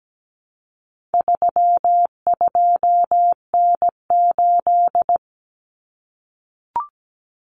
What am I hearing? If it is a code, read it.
Morse code, heard as a keyed tone, about 17 wpm, 700 Hz. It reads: 32N8
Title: Common Cut Numbers - 4-Digit 17wpm